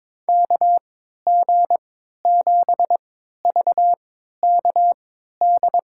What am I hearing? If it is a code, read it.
Morse code, heard as a keyed tone, about 22 wpm, spaced as for 13 wpm, 700 Hz. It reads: KG7VKD
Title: USA Call Signs - 3x Character Spacing - 22wpm